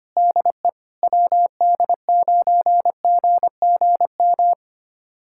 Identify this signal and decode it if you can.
Morse code, heard as a keyed tone, about 25 wpm, 700 Hz. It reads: DE WD9GGM
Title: USA Callsign DE Callsign - 6 Characters - 25wpm